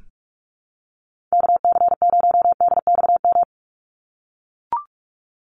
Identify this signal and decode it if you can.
Morse code, heard as a keyed tone, about 44 wpm, 700 Hz. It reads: KC0DXM
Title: USA Call Signs 45wpm